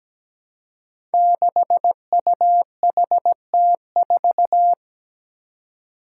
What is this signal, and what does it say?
Morse code, heard as a keyed tone, about 17 wpm, 700 Hz. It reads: 6UHT4
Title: Five Challenging Letter-Number Combinations 17wpm